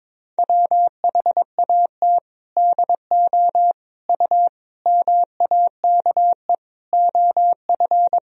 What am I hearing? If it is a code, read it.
Morse code, heard as a keyed tone, about 22 wpm, 700 Hz. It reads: WHAT DO U MAKE OF
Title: Sentences from Top 100 Words and Abbreviations 22wpm